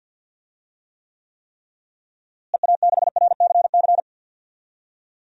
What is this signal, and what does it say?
Morse code, heard as a keyed tone, about 51 wpm, 700 Hz. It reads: EA6RCC
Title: DXCC-100 Call Signs - 6 Characters - Repeated 3x - 50wpm